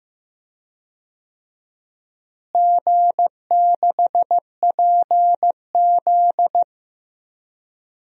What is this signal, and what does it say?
Morse code, heard as a keyed tone, about 15 wpm, 700 Hz. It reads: G6PZ
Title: DXCC-100 Call Signs - 4 Characters - Repeated 3x - 15wpm